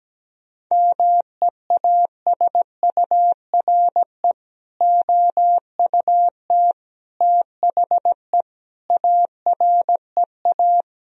Morse code, heard as a keyed tone, about 17 wpm, 700 Hz. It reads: MEASURE OUT THE AREA
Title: Sentences from Top 1000 Words - Review Part 2 - 17wpm